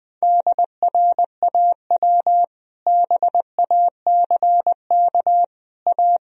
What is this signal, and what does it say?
Morse code, heard as a keyed tone, about 20 wpm, 700 Hz. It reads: DRAW BACK A
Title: Sentences from Top 1000 Words - Review Part 1 - 20wpm